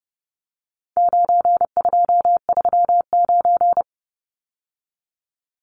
Morse code, heard as a keyed tone, about 30 wpm, 700 Hz. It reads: 9239
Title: Numbers - 4-Digit 30wpm